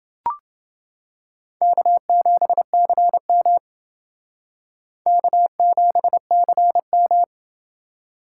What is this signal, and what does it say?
Morse code, heard as a keyed tone, about 28 wpm, 700 Hz. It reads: K7CM K7CM